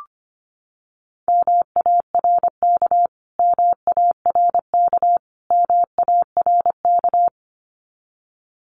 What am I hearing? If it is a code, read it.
Morse code, heard as a keyed tone, about 25 wpm, 700 Hz. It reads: MARK MARK MARK
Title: Top 300 Words Repeated 25wpm